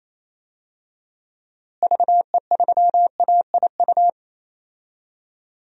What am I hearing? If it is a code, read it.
Morse code, heard as a keyed tone, about 28 wpm, 700 Hz. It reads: VE3AIU
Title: DXCC-100 Call Signs - 6 Characters - Repeated 2x - 28wpm